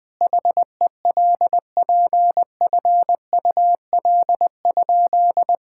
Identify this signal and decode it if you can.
Morse code, heard as a keyed tone, about 21 wpm, 700 Hz. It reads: HELPFUL?
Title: Sentences from Top 1000 Words - Review Part 6 - 20wpm